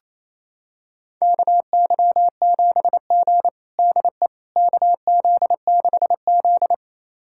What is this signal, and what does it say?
Morse code, heard as a keyed tone, about 28 wpm, 700 Hz. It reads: KY7G DE KZ6Z
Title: USA Callsign DE Callsign - 4 Characters - 28wpm